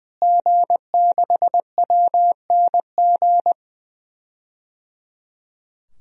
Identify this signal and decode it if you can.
Morse code, heard as a keyed tone, about 21 wpm, 700 Hz. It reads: G6WNG
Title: DXCC-100 Call Signs - 5 Characters - Repeated 3x - 20wpm